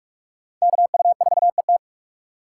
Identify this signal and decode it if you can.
Morse code, heard as a keyed tone, about 43 wpm, 700 Hz. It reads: KU4ET
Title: USA Call Signs - 5 Characters - 6x Character Spacing - 45wpm